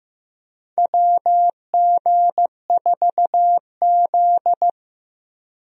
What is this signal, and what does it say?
Morse code, heard as a keyed tone, about 15 wpm, 700 Hz. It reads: WG4Z
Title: USA Call Signs - 4 Characters - 15wpm